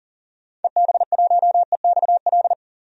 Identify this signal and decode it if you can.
Morse code, heard as a keyed tone, about 40 wpm, 700 Hz. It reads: EB1EXL
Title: DXCC-100 Call Signs - 6 Characters - 40wpm